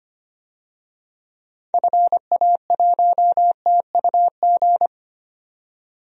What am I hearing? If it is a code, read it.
Morse code, heard as a keyed tone, about 25 wpm, 700 Hz. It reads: FA1TUG
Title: DXCC-100 Call Signs - 6 Characters - 25wpm